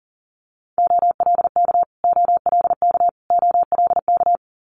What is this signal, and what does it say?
Morse code, heard as a keyed tone, about 40 wpm, 690 Hz. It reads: OLK OLK OLK
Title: Multi-letter Phonemes Repeated 3x 40wpm